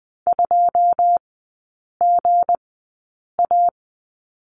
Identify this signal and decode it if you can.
Morse code, heard as a keyed tone, about 20 wpm, spaced as for 8 wpm, 700 Hz. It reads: 2GA
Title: 166 Lesson 27e - Number 7 - Three Characters 20wpm